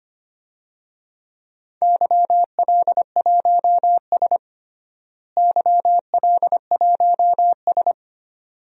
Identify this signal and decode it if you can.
Morse code, heard as a keyed tone, about 25 wpm, 700 Hz. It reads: YL1S YL1S
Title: DXCC-100 Call Signs - 4 Characters - Repeated 2x - 25wpm